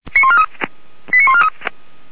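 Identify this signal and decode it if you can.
Five-tone selective call (ZVEI), heard as quick runs of five tones, tones of about 70 ms. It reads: 89143
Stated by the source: ZVEI five-tone selective call received on SDR.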